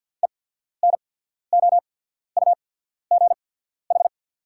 Morse code, heard as a keyed tone, about 48 wpm, spaced as for 14 wpm, 690 Hz. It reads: ENOUGH
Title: 8x Character Spacing - Top 500 Words - 50wpm